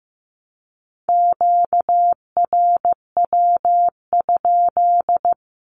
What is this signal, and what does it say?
Morse code, heard as a keyed tone, about 15 wpm, 700 Hz. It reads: QRW?
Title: Q-Codes with Spoken Meaning - 15wpm